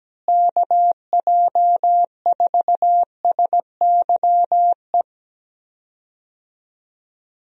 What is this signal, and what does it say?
Morse code, heard as a keyed tone, about 17 wpm, 700 Hz. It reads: KJ4SYE
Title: USA Call Signs - 6 Characters - Repeated 2x - 17wpm